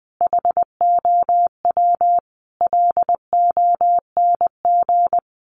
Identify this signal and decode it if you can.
Morse code, heard as a keyed tone, about 20 wpm, 700 Hz. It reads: HOW LONG
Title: Sets of 5 Words Encore2 20wpm